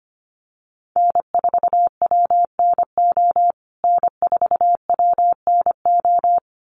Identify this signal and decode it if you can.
Morse code, heard as a keyed tone, about 25 wpm, 700 Hz. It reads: N4WNO N4WNO
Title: USA Call Signs Repeated 2x 25wpm